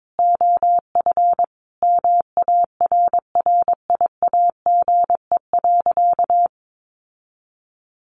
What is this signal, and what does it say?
Morse code, heard as a keyed tone, about 22 wpm, 700 Hz. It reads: OF MARRIAGE.
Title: Sentences from Top 800 Words 22wpm